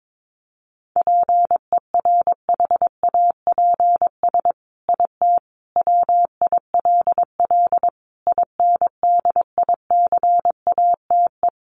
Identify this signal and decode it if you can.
Morse code, heard as a keyed tone, about 22 wpm, 700 Hz. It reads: PERHAPS IT WILL INDICATE
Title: Sentences from Top 1300 Words 22wpm